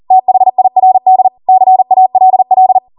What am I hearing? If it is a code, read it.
Morse code, heard as a keyed tone, about 40 wpm, 750 Hz. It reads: THIRD CALL